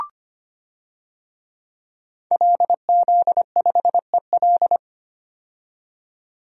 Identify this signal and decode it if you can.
Morse code, heard as a keyed tone, about 25 wpm, 700 Hz. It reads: LZ5EL
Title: DXCC-100 Call Signs - 5 Characters - 25wpm